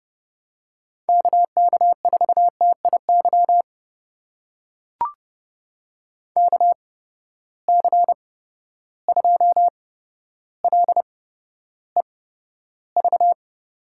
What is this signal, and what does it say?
Morse code, heard as a keyed tone, about 30 wpm, spaced as for 13 wpm, 700 Hz. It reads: KK4TIY KC2LEV
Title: USA Call Signs - 6 Characters - 8x Character Spacing - 30wpm